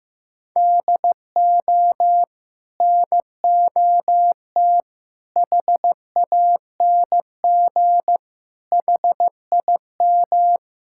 Morse code, heard as a keyed tone, about 15 wpm, 700 Hz. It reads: DO NOT HANG HIM
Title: Sentences from Top 1600 Words 15wpm